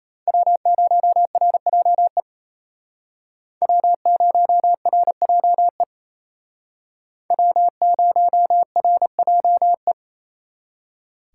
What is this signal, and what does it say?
Morse code, heard as a keyed tone, about 34 wpm, 700 Hz. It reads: W0RJE W0RJE W0RJE
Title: Speed Racing - USA Call Signs 28wpm